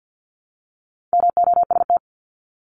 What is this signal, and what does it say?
Morse code, heard as a keyed tone, about 50 wpm, 700 Hz. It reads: MOST
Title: LY Suffix - Top 100 Words with Base Words 50wpm